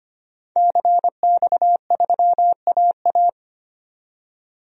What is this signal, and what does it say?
Morse code, heard as a keyed tone, about 25 wpm, 700 Hz. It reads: CX3AA
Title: DXCC-100 Call Signs - 5 Characters - 25wpm